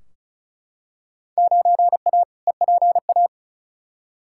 Morse code, heard as a keyed tone, about 36 wpm, 700 Hz. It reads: 9A EPA
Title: Field Day Exchanges Encore - 35wpm